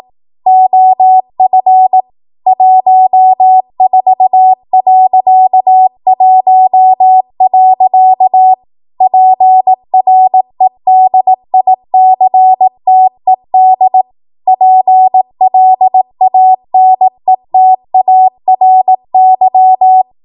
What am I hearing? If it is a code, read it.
Morse code, heard as a keyed tone, about 18 wpm, 750 Hz. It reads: OF 14.1. PREDICTED PLANETARY
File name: Morse / 18WPM